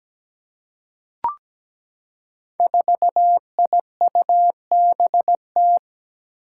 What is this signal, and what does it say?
Morse code, heard as a keyed tone, about 17 wpm, 700 Hz. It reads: 4IUBT
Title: Five Challenging Letter-Number Combinations 17wpm